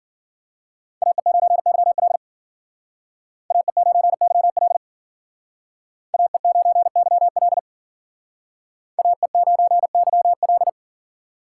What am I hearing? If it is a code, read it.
Morse code, heard as a keyed tone, about 55 wpm, 700 Hz. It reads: AE9YL AE9YL AE9YL AE9YL